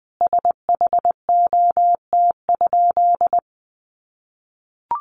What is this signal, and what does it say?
Morse code, heard as a keyed tone, about 20 wpm, 700 Hz. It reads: SHOT?
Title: Sentences from Top 1100 Words 20wpm